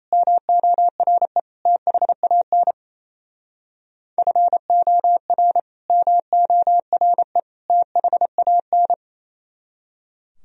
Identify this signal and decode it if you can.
Morse code, heard as a keyed tone, about 35 wpm, 700 Hz. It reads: MORE THAN FOR MORE THAN